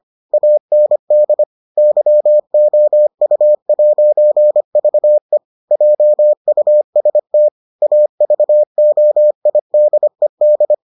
Morse code, heard as a keyed tone, about 25 wpm, 600 Hz. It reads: AND YOU'VE JUST AVOIDED